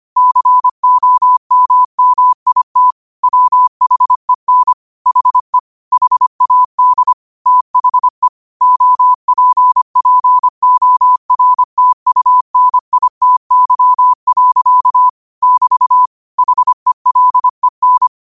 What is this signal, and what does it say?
Morse code, heard as a keyed tone, about 25 wpm, 1000 Hz. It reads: COMMIT WHEN HE HAD THE OPPORTUNITY. <BT> HELEN
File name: qotdfiles / 25wpm